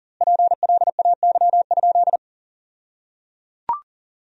Morse code, heard as a keyed tone, about 37 wpm, 700 Hz. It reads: PLAY?
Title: 2x Word-Spacing - Sentences from Top 400 Words - 40wpm